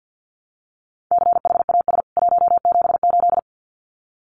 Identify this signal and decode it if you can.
Morse code, heard as a keyed tone, about 50 wpm, 700 Hz. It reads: CHAS 178
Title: CWT Exchange 50wpm